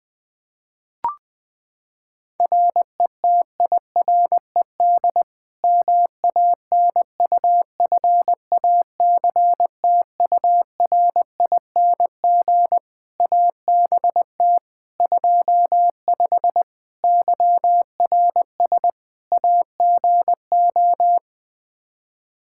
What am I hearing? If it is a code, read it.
Morse code, heard as a keyed tone, about 21 wpm, 700 Hz. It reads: RETIRED MANUFACTURING ABT 25 YRS AGO
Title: K1USN SST QSO Practice 20wpm